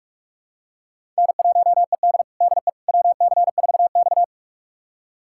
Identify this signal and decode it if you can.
Morse code, heard as a keyed tone, about 46 wpm, 700 Hz. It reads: N1ED DE WC4X
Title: USA Callsign DE Callsign - 4 Characters - 45wpm